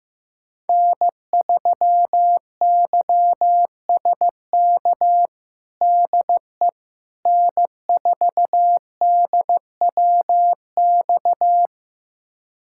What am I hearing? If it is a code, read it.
Morse code, heard as a keyed tone, about 16 wpm, 700 Hz. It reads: N3YSK DE N4DWX
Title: USA Callsign DE Callsign - 5 Characters - 15wpm